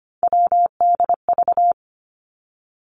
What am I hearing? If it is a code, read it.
Morse code, heard as a keyed tone, about 25 wpm, 700 Hz. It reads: WDV
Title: Three Challenging Letter-Number Combinations 25wpm